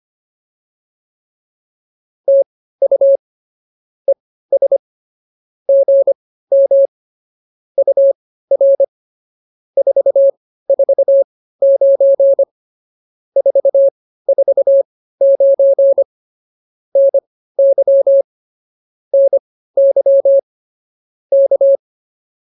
Morse code, heard as a keyed tone, about 25 wpm, spaced as for 15 wpm, 590 Hz. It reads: TU ES GM UR 449 449 NY NY K